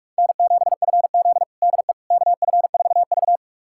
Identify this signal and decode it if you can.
Morse code, heard as a keyed tone, about 41 wpm, 700 Hz. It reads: N7FZ DE KF4V
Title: USA Callsign DE Callsign - 4 Characters - 45wpm